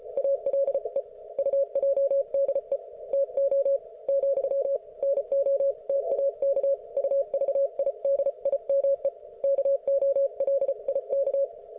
Morse code, heard as a keyed tone, about 34 wpm, 550 Hz. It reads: ALE UJDE TO , NOTAK UVIDIME KOLIK